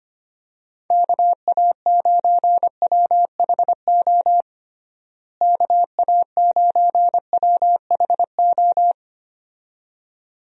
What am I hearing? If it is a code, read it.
Morse code, heard as a keyed tone, about 25 wpm, 700 Hz. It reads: KA9WHO KA9WHO
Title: USA Call Signs - 6 Characters - Repeated 2x - 25wpm